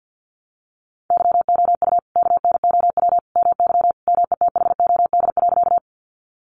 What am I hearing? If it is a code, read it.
Morse code, heard as a keyed tone, about 50 wpm, 700 Hz. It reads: YOU KNOW MY METHOD.
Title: Sentences from Top 600 Words 50wpm